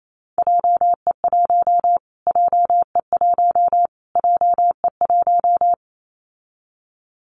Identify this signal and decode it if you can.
Morse code, heard as a keyed tone, about 28 wpm, 700 Hz. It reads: JE1 JE1 JE1
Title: Callsign Prefixes for Top 25 DXCC - Repeated 3x - 28wpm